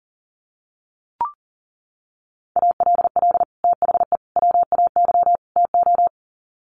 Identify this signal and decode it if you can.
Morse code, heard as a keyed tone, about 40 wpm, 700 Hz. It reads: ALL THE WAY TO
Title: Sets of 4 Words 40wpm